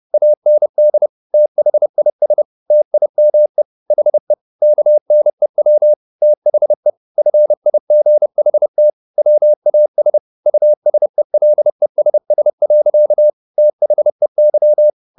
Morse code, heard as a keyed tone, about 30 wpm, 600 Hz. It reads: AND THIS TIME HE KNEW THE FIGHT WAS USELESS. THEY